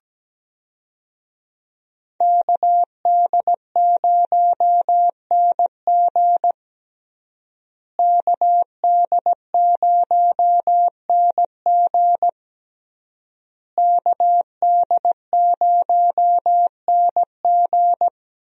Morse code, heard as a keyed tone, about 17 wpm, 700 Hz. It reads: KD0NG KD0NG KD0NG
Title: USA Call Signs - 5 Characters - Repeated 3x - 17wpm